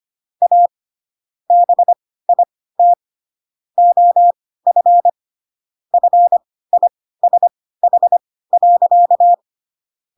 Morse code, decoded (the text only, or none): A BIT OF FISH.